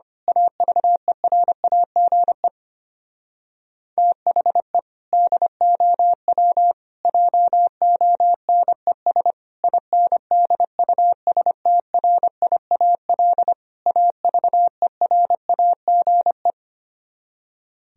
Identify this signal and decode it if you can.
Morse code, heard as a keyed tone, about 24 wpm, 700 Hz. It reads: AVERAGE THE DOW JONES INDUSTRIAL AVERAGE
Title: Speed-Racing - Sets of 5 Words 25wpm